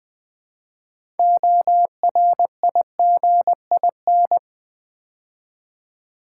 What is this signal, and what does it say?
Morse code, heard as a keyed tone, about 20 wpm, 700 Hz. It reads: ORIGIN
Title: Build A Word - 2 Letters At A Time - 8-Letter Words - 20wpm